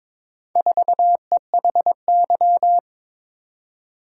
Morse code, heard as a keyed tone, about 22 wpm, 700 Hz. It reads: 4EHY
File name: practice / 22wpm / Four Letter-Number Combinations - 22wpm